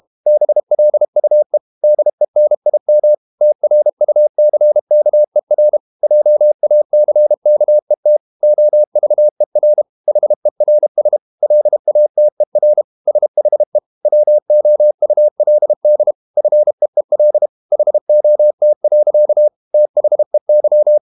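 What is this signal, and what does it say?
Morse code, heard as a keyed tone, about 32 wpm, 600 Hz. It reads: BLUE DENIM TRUCKER JACKET OVER HERS LATER SHE WOULD FEEL HOT. THEY
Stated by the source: Morse code practice at 32 wpm from CW med Gnister, reading Ken Follett's novel Never in English.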